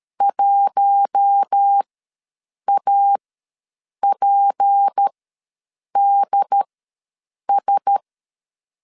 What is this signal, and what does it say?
Morse code, heard as a keyed tone, about 13 wpm, spaced as for 7 wpm, 800 Hz. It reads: 1APDS
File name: LEZIONI / 6 PKX0 / 6 P ws 12wpm 3117